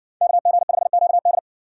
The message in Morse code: KG5QD

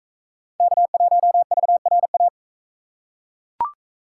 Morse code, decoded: K1VRA

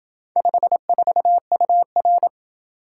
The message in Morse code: 54UR

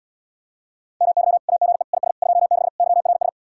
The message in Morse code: MY AGE IS 26 YRS

